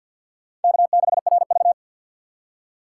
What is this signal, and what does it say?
Morse code, heard as a keyed tone, about 52 wpm, 690 Hz. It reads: K6RV
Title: USA Call Signs - 4 Characters - 6x Character Spacing - 50wpm